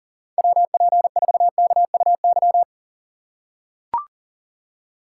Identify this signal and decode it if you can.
Morse code, heard as a keyed tone, about 38 wpm, 700 Hz. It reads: WP4KUY
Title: USA Call Signs - 6x Character Spacing - 40wpm